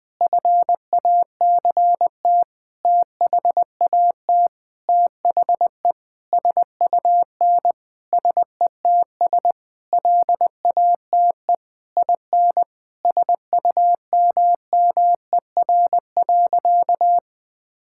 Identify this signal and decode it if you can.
Morse code, heard as a keyed tone, about 21 wpm, 700 Hz. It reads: FACT THAT THE SUN SETS LATE IN SUMMER.
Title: Sentences with 8-Word Phrases - 20wpm